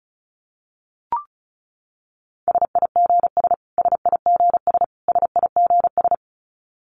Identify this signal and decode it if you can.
Morse code, heard as a keyed tone, about 34 wpm, 700 Hz. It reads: SIGS SIGS SIGS
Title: Abbreviations Repeated 3x - 35wpm